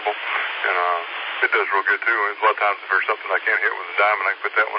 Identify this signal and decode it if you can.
Emergency Alert System (SAME) tones, heard as harsh buzzing bursts, with speech. It is heard from 0.3 to 3.8 s.